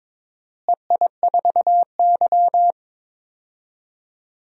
Morse code, heard as a keyed tone, about 22 wpm, 700 Hz. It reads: EI4Y
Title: DXCC-100 Call Signs - 4 Characters - Repeated 2x - 22wpm